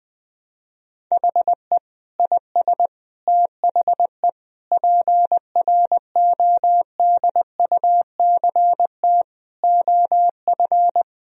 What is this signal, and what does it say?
Morse code, heard as a keyed tone, about 20 wpm, 700 Hz. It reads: HE IS THE PRODUCT OF
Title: Sentences from Top 1000 Words - Review Part 2 - 20wpm